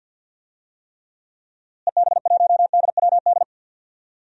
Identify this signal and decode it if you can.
Morse code, heard as a keyed tone, about 46 wpm, 700 Hz. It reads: EB1DWD
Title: DXCC-100 Call Signs - 6 Characters - Repeated 3x - 50wpm